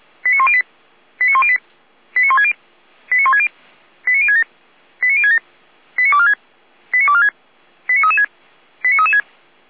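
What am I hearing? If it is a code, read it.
Five-tone selective call (ZVEI), heard as quick runs of five tones, tones of about 80 ms. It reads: 89108 89108 89160 89160 89067 89067 89246 89246 89336 89336
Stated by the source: Received on an SDR, ZVEI five-tone selective call.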